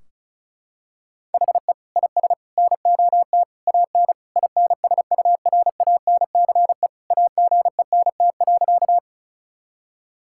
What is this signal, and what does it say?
Morse code, heard as a keyed tone, about 36 wpm, 700 Hz. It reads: HE IS NOT AN INSURANCE AGENT.
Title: Sentences from Top 1600 Words 35wpm